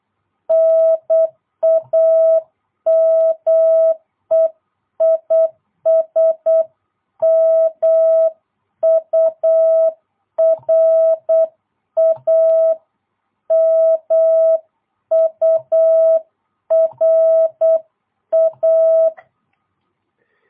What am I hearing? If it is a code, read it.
Morse code, heard as a keyed tone, about 8 wpm, 660 Hz. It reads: NAME IS MURA MURA